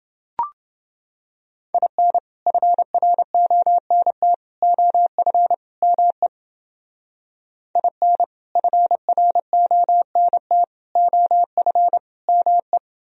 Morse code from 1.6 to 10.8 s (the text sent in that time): IN FRONT OF ME IN FRONT